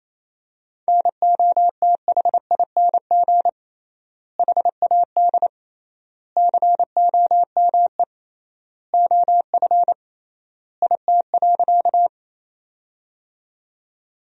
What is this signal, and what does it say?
Morse code, heard as a keyed tone, about 28 wpm, 700 Hz. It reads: NOTHING HAD COME OF IT.
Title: 3x Word-Spacing - Sentences from Top 500 Words - 28wpm